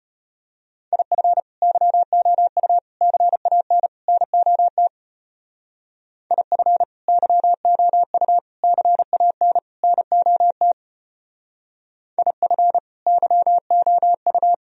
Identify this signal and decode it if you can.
Morse code, heard as a keyed tone, about 31 wpm, 700 Hz. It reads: IF YOU CAN NOT IF YOU CAN NOT IF YOU
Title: Speed-Racing - Sets of 4 Words 25wpm